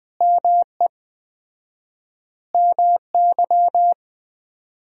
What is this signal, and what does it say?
Morse code, heard as a keyed tone, about 20 wpm, 700 Hz. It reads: ME MY
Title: 4x Word-Spacing - Sentences from Top 200 Words - 20wpm